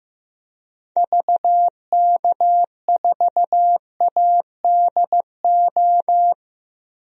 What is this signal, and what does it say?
Morse code, heard as a keyed tone, about 15 wpm, 700 Hz. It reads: VK4ADO
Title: DXCC-100 Call Signs - 6 Characters - 15wpm